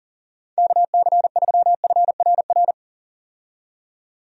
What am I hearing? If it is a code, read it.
Morse code, heard as a keyed tone, about 40 wpm, 700 Hz. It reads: KC3FRR
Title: USA Call Signs - 6 Characters - 40wpm